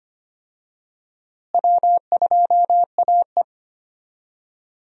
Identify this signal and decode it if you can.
Morse code, heard as a keyed tone, about 25 wpm, 700 Hz. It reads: W2AE
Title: USA Call Signs - 4 Characters - Repeated 2x - 25wpm